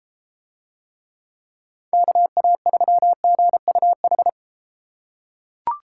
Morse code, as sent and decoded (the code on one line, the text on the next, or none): -.- .- ...-- --. ..- ....
KA3GUH